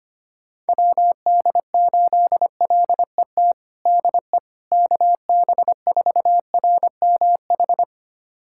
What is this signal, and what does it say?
Morse code, heard as a keyed tone, about 25 wpm, 700 Hz. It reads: WD8LET DE KB4RMH
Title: USA Callsign DE Callsign 25wpm